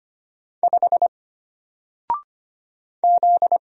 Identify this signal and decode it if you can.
Morse code, heard as a keyed tone, about 25 wpm, 700 Hz. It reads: 5 Z
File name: Single-Letter-Number-25wpm